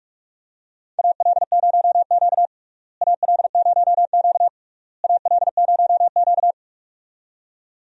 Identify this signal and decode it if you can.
Morse code, heard as a keyed tone, about 41 wpm, 700 Hz. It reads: AL0Q AL0Q AL0Q
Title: USA Call Signs - 4 Characters - Repeated 3x - 45wpm